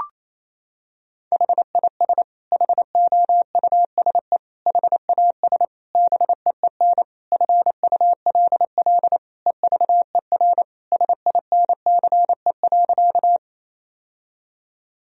Morse code, heard as a keyed tone, about 28 wpm, 700 Hz. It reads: HIS HOUSE HAS BEEN FULL EVER SINCE.
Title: Sentences from Top 1000 Words - Review Part 3 - 28wpm